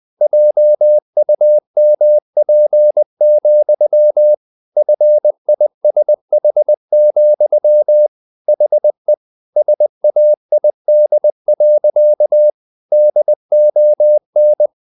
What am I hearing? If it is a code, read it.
Morse code, heard as a keyed tone, about 20 wpm, 600 Hz. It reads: JUMP, FISH, HE SAID. DON
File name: the old man and the sea part 2 20wpm